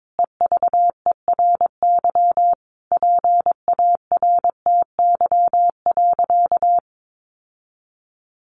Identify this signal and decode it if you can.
Morse code, heard as a keyed tone, about 22 wpm, 700 Hz. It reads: EVERY PARTY.
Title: Sentences from Top 1500 Words 22wpm